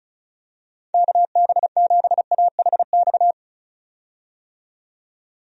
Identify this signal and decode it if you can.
Morse code, heard as a keyed tone, about 36 wpm, 700 Hz. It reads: KB7AHX
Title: USA Call Signs - 6 Characters - 35wpm